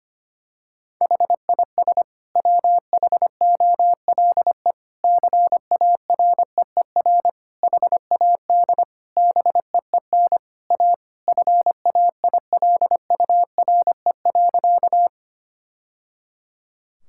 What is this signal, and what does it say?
Morse code, heard as a keyed tone, about 25 wpm, 700 Hz. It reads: HIS WHOLE CAREER HAD BEEN A FAILURE.
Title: Sentences from Top 1000 Words - Review Part 6 - 25wpm